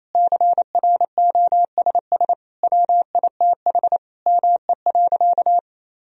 Morse code, heard as a keyed tone, about 28 wpm, 700 Hz. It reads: CROSS WITH ME.
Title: Sentences from Top 1000 Words - Review Part 1 - 28wpm